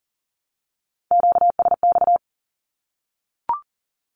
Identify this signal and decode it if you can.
Morse code, heard as a keyed tone, about 39 wpm, 700 Hz. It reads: QSX
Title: Q-Codes Repeated 3x 40wpm